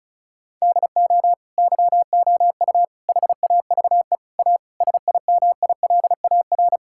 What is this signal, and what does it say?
Morse code, heard as a keyed tone, about 34 wpm, 700 Hz. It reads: DO YOU HAVE A SIMILAR
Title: Sentences from Top 1000 Words - Review Part 5 - 35wpm